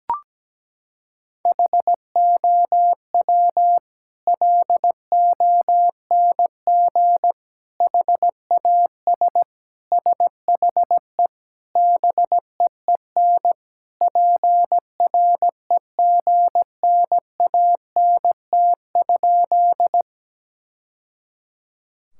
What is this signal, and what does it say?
Morse code, heard as a keyed tone, about 17 wpm, 700 Hz. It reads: HOW LONG HAS SHE BEEN PREGNANT?
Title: Sentences from Top 1000 Words - Review Part 6 - 17wpm